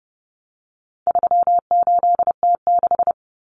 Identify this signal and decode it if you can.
Morse code, heard as a keyed tone, about 30 wpm, 700 Hz. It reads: 38T6
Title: Common Cut Numbers - 4-Digit 30wpm